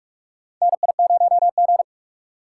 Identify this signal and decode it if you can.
Morse code, heard as a keyed tone, about 44 wpm, 700 Hz. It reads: NI0G
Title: USA Call Signs - 4 Characters - 3x Character Spacing - 45wpm